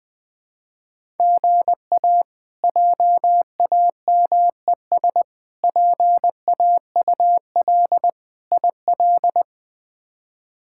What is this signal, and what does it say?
Morse code, heard as a keyed tone, about 21 wpm, 700 Hz. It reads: GA JAMES PAUL IL
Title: K1USN SST QSO Practice - 20wpm